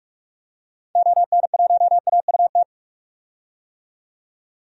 Morse code, heard as a keyed tone, about 42 wpm, 700 Hz. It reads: ON1AUT
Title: DXCC-100 Call Signs - 6 Characters - 45wpm